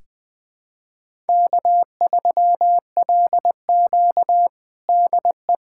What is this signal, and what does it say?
Morse code, heard as a keyed tone, about 20 wpm, 700 Hz. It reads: K3LQ DE
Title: USA Callsign DE Callsign - 4 Characters - 20wpm